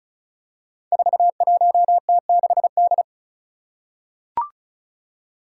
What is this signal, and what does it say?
Morse code, heard as a keyed tone, about 34 wpm, 700 Hz. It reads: 41T6D
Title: Five Challenging Letter-Number Combinations 35wpm